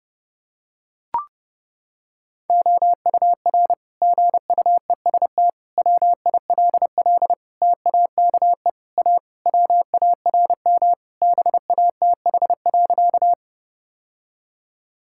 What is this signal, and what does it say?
Morse code, heard as a keyed tone, about 30 wpm, 700 Hz. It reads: OUR GUEST WILL TAKE A WARM BATH.
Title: Sentences from Top 1600 Words 30wpm